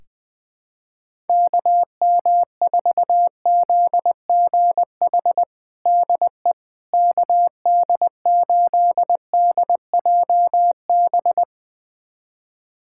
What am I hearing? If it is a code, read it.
Morse code, heard as a keyed tone, about 20 wpm, 700 Hz. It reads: KM4ZGH DE KD8DJB
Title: USA Callsign DE Callsign 20wpm